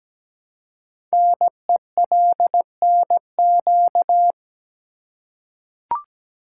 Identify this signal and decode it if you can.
Morse code, heard as a keyed tone, about 17 wpm, 700 Hz. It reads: NELNQ
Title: Code Groups - 5 Letters - 17wpm